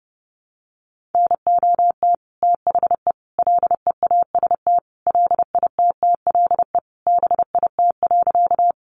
Morse code, heard as a keyed tone, about 30 wpm, 700 Hz. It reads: NOT THE LEAST LITTLE BIT.
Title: Sentences from Top 900 Words 30wpm